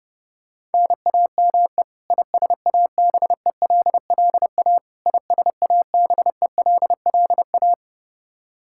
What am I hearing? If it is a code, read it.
Morse code, heard as a keyed tone, about 30 wpm, 700 Hz. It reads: NAME ISABELLA ISABELLA
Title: Practice QSO - Exchange 1 - Beginning - 30wpm